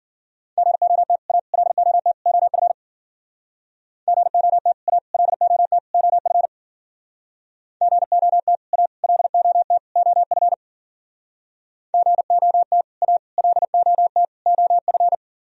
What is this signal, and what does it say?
Morse code, heard as a keyed tone, about 51 wpm, 700 Hz. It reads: GOT A LOT OF GOT A LOT OF GOT A LOT OF GOT A LOT OF